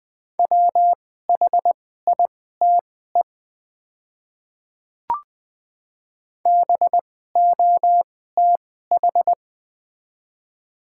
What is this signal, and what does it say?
Morse code, heard as a keyed tone, about 20 wpm, spaced as for 14 wpm, 700 Hz. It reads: WHITE BOTH
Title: 2x Character Spacing - Top 500 Words - 20wpm